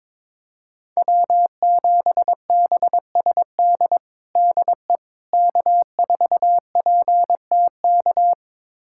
Morse code, heard as a keyed tone, about 22 wpm, 700 Hz. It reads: W7BSD DE K4PTK
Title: USA Callsign DE Callsign - 5 Characters - 22wpm